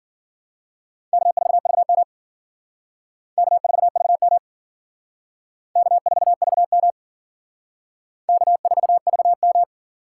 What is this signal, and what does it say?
Morse code, heard as a keyed tone, about 50 wpm, 700 Hz. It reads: K4VM K4VM K4VM K4VM